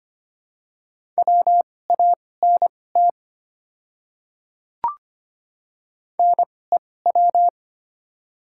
Morse code, heard as a keyed tone, about 25 wpm, spaced as for 18 wpm, 700 Hz. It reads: WANT NEW
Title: 2x Character Spacing - Top 200 Words - 25wpm